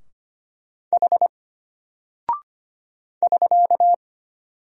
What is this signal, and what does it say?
Morse code, heard as a keyed tone, about 25 wpm, 700 Hz. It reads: H <SK>
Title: Single Character (Letter:Number:Punctuation:Prosign) Alt 25wpm